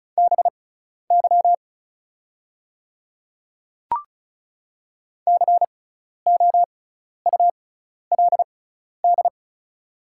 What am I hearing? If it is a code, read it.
Morse code, heard as a keyed tone, about 36 wpm, spaced as for 12 wpm, 700 Hz. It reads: BY COULD
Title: 6x Character Spacing - Top 100 Words - 35wpm